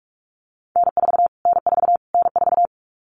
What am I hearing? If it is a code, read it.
Morse code, heard as a keyed tone, about 46 wpm, 700 Hz. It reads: N4 N4 N4
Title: Callsign Prefixes for Top 25 DXCC - Repeated 3x - 45wpm